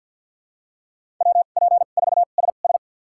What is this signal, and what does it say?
Morse code, heard as a keyed tone, about 50 wpm, spaced as for 36 wpm, 700 Hz. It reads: WP4SS
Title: USA Call Signs - 5 Characters - 2x Character Spacing - 50wpm